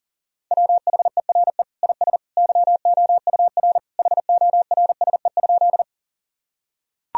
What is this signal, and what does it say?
Morse code, heard as a keyed tone, about 40 wpm, 700 Hz. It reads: WHERE IS YOUR HORSE?
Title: Sentences from Top 1000 Words - Review Part 1 - 40wpm